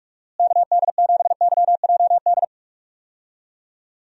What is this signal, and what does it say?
Morse code, heard as a keyed tone, about 46 wpm, 700 Hz. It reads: KD7YJD
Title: USA Call Signs - 6 Characters - Repeated 2x - 45wpm